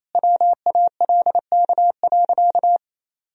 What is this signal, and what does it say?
Morse code, heard as a keyed tone, about 28 wpm, 700 Hz. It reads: WALK.
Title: Sentences from Top 1000 Words - Review Part 2 - 28wpm